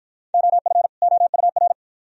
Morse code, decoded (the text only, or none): OF OUR